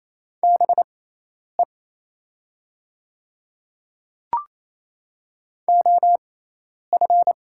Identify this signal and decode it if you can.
Morse code, heard as a keyed tone, about 28 wpm, spaced as for 10 wpm, 700 Hz. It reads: BE OF
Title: 6x Character Spacing - Top 200 Words - 28wpm